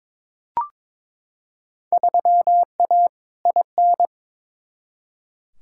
Morse code, heard as a keyed tone, about 22 wpm, 700 Hz. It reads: 3A IN
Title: Field Day Exchanges - 22wpm